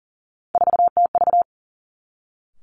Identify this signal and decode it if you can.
Morse code, heard as a keyed tone, about 41 wpm, 700 Hz. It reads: 4TV
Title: Three Challenging Letter-Number Combinations 40wpm